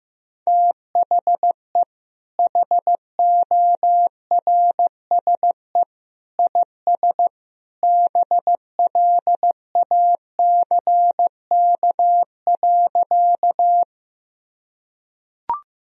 Morse code, heard as a keyed tone, about 16 wpm, 700 Hz. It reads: THE HORSE IS BLACK.
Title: Sentences from Top 1000 Words - Review Part 2 - 15wpm